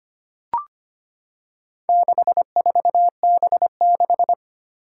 Morse code, heard as a keyed tone, about 25 wpm, 700 Hz. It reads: 64B6
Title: Four Challenging Letter-Number Combinations 25wpm